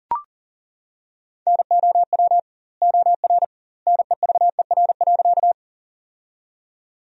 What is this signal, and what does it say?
Morse code, heard as a keyed tone, about 40 wpm, 700 Hz. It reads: NOW OR NEVER.
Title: 2x Word-Spacing - Sentences from Top 300 Words - 40wpm